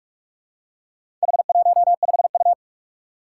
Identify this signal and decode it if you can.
Morse code, heard as a keyed tone, about 44 wpm, 700 Hz. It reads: H15U
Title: Four Challenging Letter-Number Combinations 45wpm